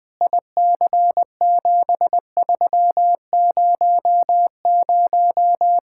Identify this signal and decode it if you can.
Morse code, heard as a keyed tone, about 20 wpm, 700 Hz. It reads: IC7300
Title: K1USN SST QSO Practice 20wpm